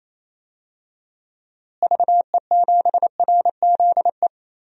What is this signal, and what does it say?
Morse code, heard as a keyed tone, about 28 wpm, 700 Hz. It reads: VE7RZE
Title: DXCC-100 Call Signs - 6 Characters - Repeated 3x - 28wpm